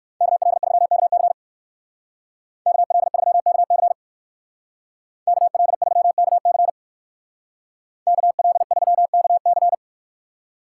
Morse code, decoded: KL3KC KL3KC KL3KC KL3KC

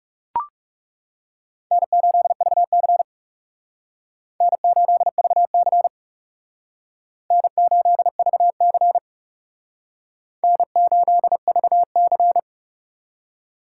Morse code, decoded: N8VC N8VC N8VC N8VC

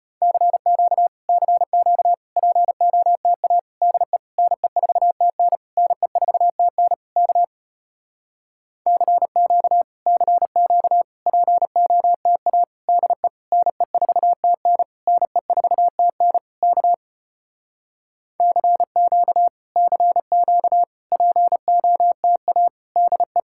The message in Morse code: CQ CQ POTA DE NE4TN NE4TN K CQ CQ POTA DE NE4TN NE4TN K CQ CQ POTA DE

